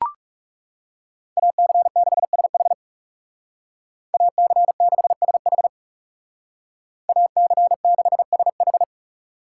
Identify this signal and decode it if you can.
Morse code, heard as a keyed tone, about 37 wpm, 700 Hz. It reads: AC6SH AC6SH AC6SH
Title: Speed Racing - USA 2x2 Call Signs 30wpm